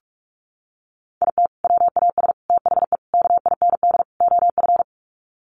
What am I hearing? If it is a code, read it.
Morse code, heard as a keyed tone, about 44 wpm, 700 Hz. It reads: IT WAS THE KIND OF
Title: Sets of 5 Words Encore2 45wpm